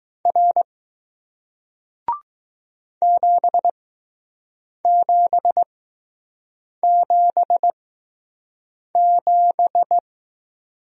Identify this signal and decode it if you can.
Morse code, heard as a keyed tone, about 20 wpm, spaced as for 6 wpm, 700 Hz. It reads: R 7777